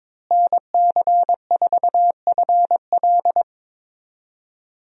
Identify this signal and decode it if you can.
Morse code, heard as a keyed tone, about 22 wpm, 700 Hz. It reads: NC4FL
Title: USA Call Signs - 5 Characters - Repeated 2x - 22wpm